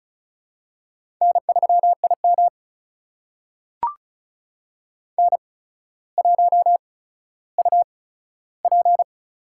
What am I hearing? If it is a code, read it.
Morse code, heard as a keyed tone, about 36 wpm, spaced as for 15 wpm, 700 Hz. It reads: N3IM N1UP
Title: USA Call Signs - 4 Characters - 8x Character Spacing - 35wpm